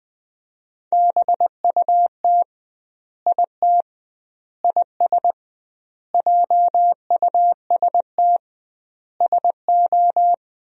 Morse code, heard as a keyed tone, about 21 wpm, 700 Hz. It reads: BUT IT IS JUST SO
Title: 2x Word-Spacing - Sentences from Top 200 Words - 20wpm